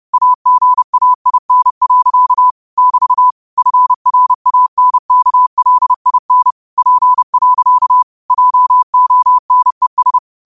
Morse code, heard as a keyed tone, about 30 wpm, 1000 Hz. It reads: AGAIN. <BT> FRANKLIN P. JONES